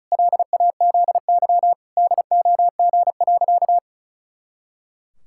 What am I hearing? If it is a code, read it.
Morse code, heard as a keyed tone, about 36 wpm, 700 Hz. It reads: LAZY DOG.
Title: Pangram Challenge v2 - 35wpm